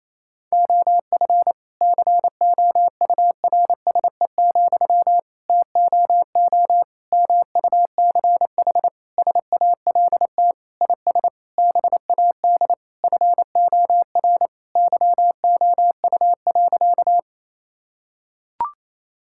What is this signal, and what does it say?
Morse code, heard as a keyed tone, about 27 wpm, 700 Hz. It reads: OF COURSE, TOO MUCH SALT IS BAD FOR YOU.
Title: Sentences from Top 1000 Words - Review Part 5 - 28wpm